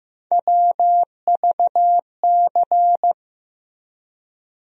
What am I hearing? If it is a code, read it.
Morse code, heard as a keyed tone, about 15 wpm, 700 Hz. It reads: WVC
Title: USA Call Sign Postfixes - 3 Letters 15wpm